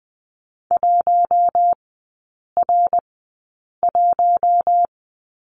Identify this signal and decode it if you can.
Morse code, heard as a keyed tone, about 20 wpm, spaced as for 8 wpm, 700 Hz. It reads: 1R1
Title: 074 Lesson 13e - Letter L - Three Characters 20wpm